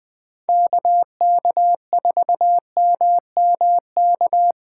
Morse code, heard as a keyed tone, about 20 wpm, 700 Hz. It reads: KK4MMK
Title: USA Call Signs - 6 Characters - 2x Character Spacing - 20wpm